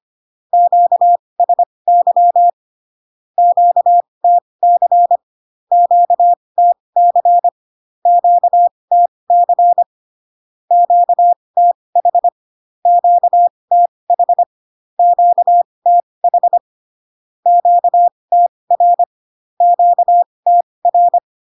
Morse code, heard as a keyed tone, about 25 wpm, spaced as for 20 wpm, 700 Hz. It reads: QSY QTC QTC QTC QTH QTH QTH QTR QTR
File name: cw / Q-codes most common repeated 3 times 15 25wpm ik1hge